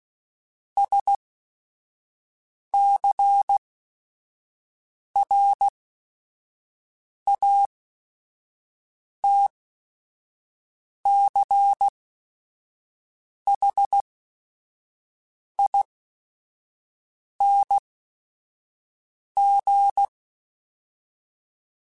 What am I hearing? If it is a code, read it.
Morse code, heard as a keyed tone, about 16 wpm, spaced as for 5 wpm, 800 Hz. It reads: SCRATCHING